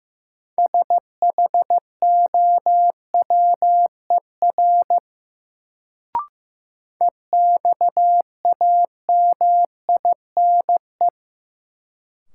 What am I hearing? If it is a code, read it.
Morse code, heard as a keyed tone, about 15 wpm, 700 Hz. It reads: SHOWER EXAMINE
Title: Sentences from Top 1600 Words - New Word Drill - 15wpm